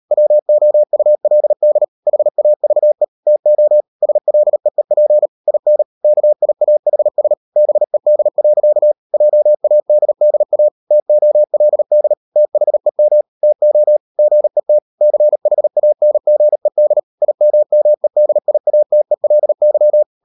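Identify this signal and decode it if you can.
Morse code, heard as a keyed tone, about 38 wpm, 600 Hz. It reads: WOULD HAVE TO SLEEP IN KIAHS BED. JADDA TOLD THEM TO GET CHANGED IMMEDIATELY
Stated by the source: Morse code practice at 38 WPM, sending Ken Follett's novel Never in English, from CW med Gnister.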